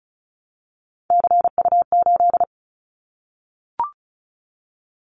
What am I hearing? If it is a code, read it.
Morse code, heard as a keyed tone, about 36 wpm, 700 Hz. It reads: CU8
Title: Callsign Prefixes for Top 100 DXCC - 35wpm